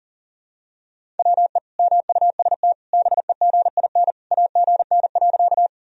Morse code, heard as a keyed tone, about 40 wpm, 700 Hz. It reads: WE MUST BEGIN AGN.
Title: Sentences from Top 300 Words and Abbreviations 40wpm